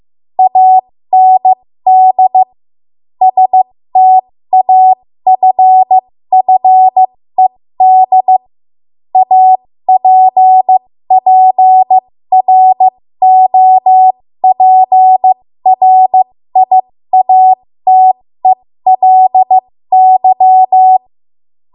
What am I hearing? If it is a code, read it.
Morse code, heard as a keyed tone, about 15 wpm, spaced as for 13 wpm, 750 Hz. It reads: AND STAFFED APPROPRIATELY
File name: Morse / 13WPM